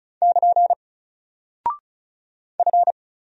Morse code, heard as a keyed tone, about 36 wpm, 700 Hz. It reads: <KN> F
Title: Single Character (Letter:Number:Punctuation:Prosign) 35wpm